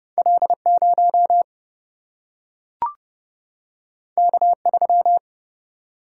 Morse code, heard as a keyed tone, about 29 wpm, 700 Hz. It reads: L0 K3
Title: Single-Letter Single-Number 30wpm